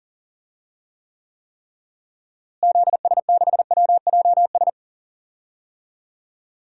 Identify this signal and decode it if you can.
Morse code, heard as a keyed tone, about 40 wpm, 700 Hz. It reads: ZS6WJS
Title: DXCC-100 Call Signs - 6 Characters - Repeated 3x - 40wpm